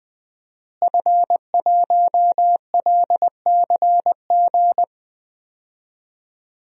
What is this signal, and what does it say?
Morse code, heard as a keyed tone, about 19 wpm, 700 Hz. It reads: F1LCG
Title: DXCC-100 Call Signs - 5 Characters - Repeated 3x - 20wpm